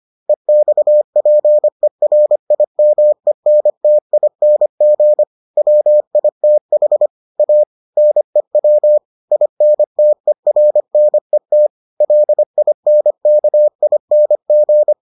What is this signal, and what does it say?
Morse code, heard as a keyed tone, about 25 wpm, 600 Hz. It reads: EXPERIMENTING WITH A NEW INTERNET LINKING